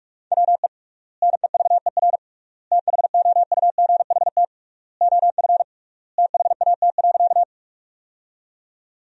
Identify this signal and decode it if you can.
Morse code, heard as a keyed tone, about 47 wpm, 700 Hz. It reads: WE NEVER THOUGHT OF THAT.
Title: 3x Word-Spacing - Sentences from Top 300 Words - 45wpm